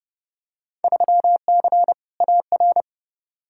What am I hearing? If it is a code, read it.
Morse code, heard as a keyed tone, about 29 wpm, 700 Hz. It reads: 3C AR
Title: Field Day Exchanges Encore - 30wpm